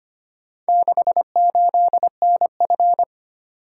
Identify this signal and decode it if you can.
Morse code, heard as a keyed tone, about 25 wpm, 700 Hz. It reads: 68NF
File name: practice / 25wpm / Four Letter-Number Combinations - 25wpm